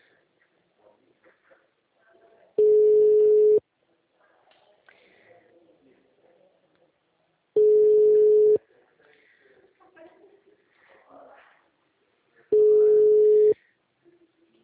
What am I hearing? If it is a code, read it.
Ringback tone (Europe, 425 Hz).